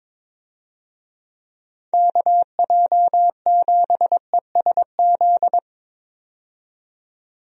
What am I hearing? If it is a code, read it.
Morse code, heard as a keyed tone, about 22 wpm, 700 Hz. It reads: KJ7ESZ
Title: USA Call Signs - 6 Characters - 22wpm